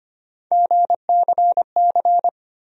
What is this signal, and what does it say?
Morse code, heard as a keyed tone, about 25 wpm, 700 Hz. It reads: GCC
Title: USA Call Sign Postfixes - 3 Letters 25wpm